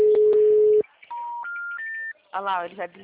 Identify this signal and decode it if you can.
Special information tone, with speech.